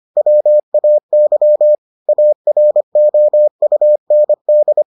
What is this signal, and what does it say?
Morse code, heard as a keyed tone, about 25 wpm, 600 Hz. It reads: WAY AROUND